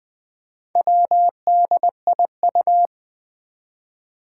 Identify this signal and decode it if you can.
Morse code, heard as a keyed tone, about 20 wpm, 700 Hz. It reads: WDIU
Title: Four Challenging Letter-Number Combinations 20wpm